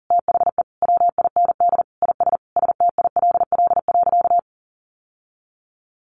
Morse code, heard as a keyed tone, about 40 wpm, 700 Hz. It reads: THE WIND IS STILL.
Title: Sentences from Top 1200 Words 40wpm